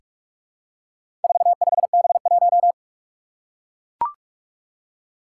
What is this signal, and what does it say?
Morse code, heard as a keyed tone, about 44 wpm, 700 Hz. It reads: 45B1
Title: Four Challenging Letter-Number Combinations 45wpm